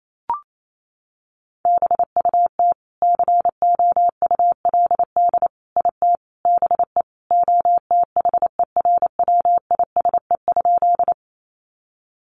Morse code, heard as a keyed tone, about 28 wpm, 700 Hz. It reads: BUT COULD IT BE OTHERWISE?
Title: Sentences from Top 1200 Words 28wpm